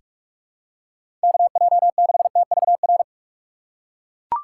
Morse code, heard as a keyed tone, about 42 wpm, 700 Hz. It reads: KJ6TVR
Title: USA Call Signs - 6 Characters - 3x Character Spacing - 45wpm